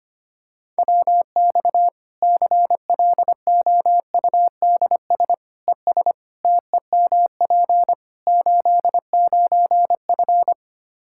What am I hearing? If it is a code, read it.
Morse code, heard as a keyed tone, about 25 wpm, 700 Hz. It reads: WX CLOUDS ES TEMP 89F
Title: Practice QSO - Exchange 2 - Info - 25wpm